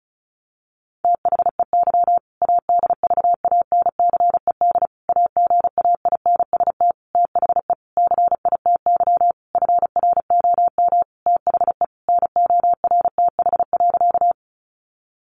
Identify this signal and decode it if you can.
Morse code, heard as a keyed tone, about 34 wpm, 700 Hz. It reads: THEY ADVANCED AGAINST THE CITY FROM THE NORTH.
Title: Sentences from Top 1200 Words 35wpm